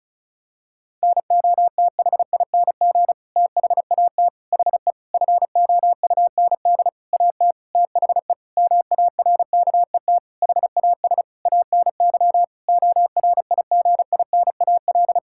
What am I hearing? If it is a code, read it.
Morse code, heard as a keyed tone, about 36 wpm, 700 Hz. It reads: NOTHING THAT HE FOUND AT THE MARKET HAS ANY ORIGINAL
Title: Sentences from Top 1000 Words - Review Part 6 - 35wpm